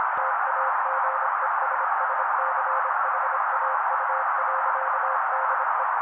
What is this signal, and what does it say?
Morse code, heard as a keyed tone, about 26 wpm, 530 Hz. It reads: TAGESSCHAU.DE
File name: daily / 25wpm noise